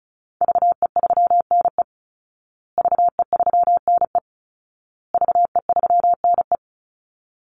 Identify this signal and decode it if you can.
Morse code, heard as a keyed tone, about 34 wpm, 700 Hz. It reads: VE3NE VE3NE VE3NE
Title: Call Signs Repeated 2x - 35wpm